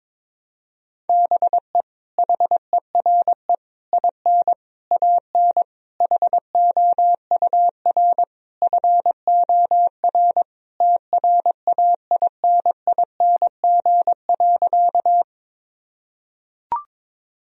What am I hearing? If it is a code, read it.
Morse code, heard as a keyed tone, about 22 wpm, 700 Hz. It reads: BE HERE IN AN HOUR FOR TRAINING.
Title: Sentences from Top 1000 Words - Review Part 4 - 22wpm